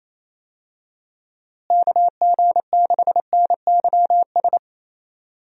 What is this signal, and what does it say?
Morse code, heard as a keyed tone, about 28 wpm, 700 Hz. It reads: KG6NYS
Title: USA Call Signs - 6 Characters - Repeated 3x - 28wpm